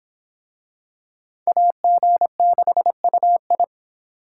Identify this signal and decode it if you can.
Morse code, heard as a keyed tone, about 25 wpm, 700 Hz. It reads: AG6UI